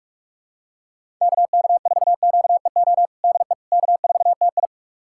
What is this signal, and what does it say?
Morse code, heard as a keyed tone, about 46 wpm, 700 Hz. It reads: KK4QEO DE K4TI
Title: USA Callsign DE Callsign 45wpm